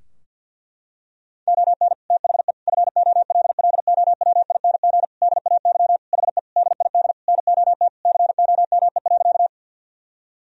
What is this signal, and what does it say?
Morse code, heard as a keyed tone, about 50 wpm, 700 Hz. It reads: ON THE FOLLOWING DAY HE DID NOT COME.
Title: Sentences from Top 1000 Words - Review Part 4 - 50wpm